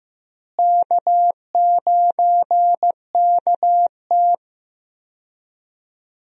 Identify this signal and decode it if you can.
Morse code, heard as a keyed tone, about 15 wpm, 700 Hz. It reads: K9KT
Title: USA Call Signs - 4 Characters - Repeated 2x - 15wpm